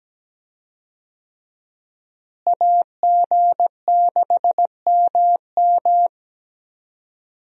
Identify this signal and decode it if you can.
Morse code, heard as a keyed tone, about 17 wpm, 700 Hz. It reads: AG6MM
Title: USA Call Signs - 5 Characters - Repeated 3x - 17wpm